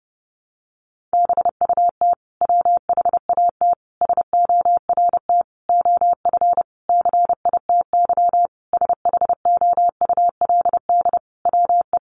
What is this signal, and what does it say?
Morse code, heard as a keyed tone, about 30 wpm, 700 Hz. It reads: BUT WHAT SORT OF CITY SHOULD WE
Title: Sentences from Top 1300 Words 30wpm